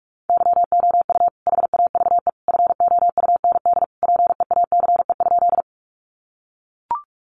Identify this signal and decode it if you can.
Morse code, heard as a keyed tone, about 44 wpm, 700 Hz. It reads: YOU HAVE FOUND PEACE?
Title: Sentences from Top 1500 Words 45wpm